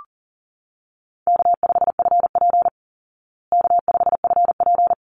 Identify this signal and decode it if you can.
Morse code, heard as a keyed tone, about 40 wpm, 700 Hz. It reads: K5FP K5FP
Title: Easy Call Signs Repeated 2x - 40wpm